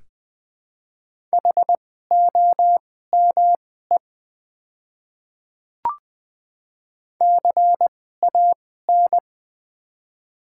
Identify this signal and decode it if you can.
Morse code, heard as a keyed tone, about 20 wpm, spaced as for 14 wpm, 700 Hz. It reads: HOME CAN
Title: 2x Character Spacing - Top 300 Words - 20wpm